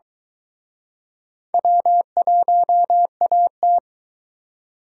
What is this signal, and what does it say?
Morse code, heard as a keyed tone, about 23 wpm, 700 Hz. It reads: W1AT